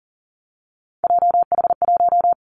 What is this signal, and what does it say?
Morse code, heard as a keyed tone, about 40 wpm, 700 Hz. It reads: JH1
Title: Callsign Prefixes for Top 25 DXCC - 40wpm